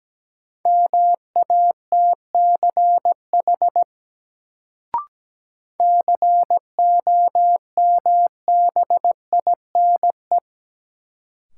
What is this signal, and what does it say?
Morse code, heard as a keyed tone, about 17 wpm, 700 Hz. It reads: MATCH COMBINE
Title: Sentences from Top 1600 Words - New Word Drill - 17wpm